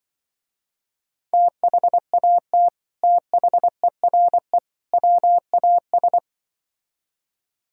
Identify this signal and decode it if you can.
Morse code, heard as a keyed tone, about 24 wpm, 700 Hz. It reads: THAT THERE WAS
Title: Speed-Racing - Sets of 3 Words 20wpm